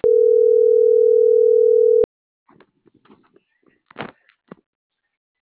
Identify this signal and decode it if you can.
Ringback tone (North America), 440 + 480 Hz.